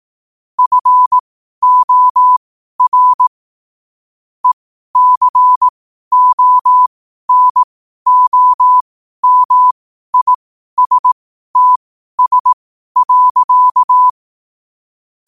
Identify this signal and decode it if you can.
Morse code, heard as a keyed tone, about 18 wpm, spaced as for 12 wpm, 1000 Hz. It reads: FOR ECONOMISTS.
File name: qotdfiles / 12wpm